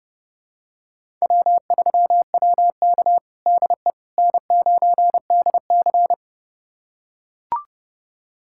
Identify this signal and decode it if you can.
Morse code, heard as a keyed tone, about 32 wpm, 700 Hz. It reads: W3WK DE N9DC
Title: USA Callsign DE Callsign - 4 Characters - 30wpm